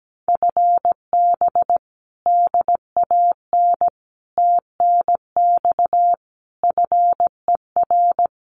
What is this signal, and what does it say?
Morse code, heard as a keyed tone, about 17 wpm, 700 Hz. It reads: FB DAN TNX FER
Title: Common QSO Phrases Encore 17wpm